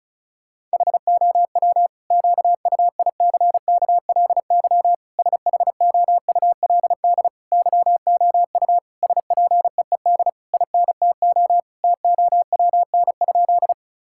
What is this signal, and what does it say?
Morse code, heard as a keyed tone, about 34 wpm, 700 Hz. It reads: HOW QUICKLY SHOULD YOU SPEED INTO TOWN?
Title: Sentences from Top 1000 Words - Review Part 5 - 35wpm